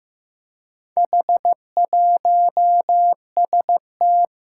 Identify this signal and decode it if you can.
Morse code, heard as a keyed tone, about 15 wpm, 700 Hz. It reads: H1ST
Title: Four Challenging Letter-Number Combinations 15wpm